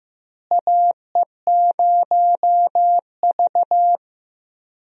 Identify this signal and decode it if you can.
Morse code, heard as a keyed tone, about 15 wpm, 700 Hz. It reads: AE0V
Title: USA Call Signs - 4 Characters - Repeated 3x - 15wpm